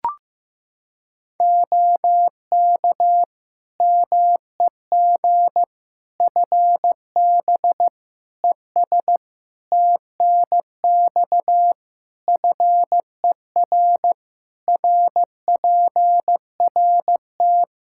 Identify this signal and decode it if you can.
Morse code, heard as a keyed tone, about 15 wpm, 700 Hz. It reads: OK MEG FB ES TNX FER RPRT
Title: Practice QSO - Exchange 2 - Info - 15wpm